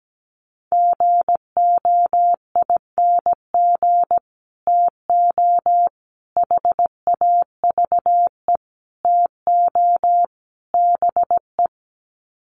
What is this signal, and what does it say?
Morse code, heard as a keyed tone, about 17 wpm, 700 Hz. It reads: GOING TO HAVE TO BE
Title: Sets of 5 Words 17wpm